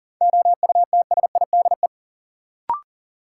Morse code, heard as a keyed tone, about 40 wpm, 700 Hz. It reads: OUTSIDE
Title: Sentences from Top 900 Words - New Word Drill - 40wpm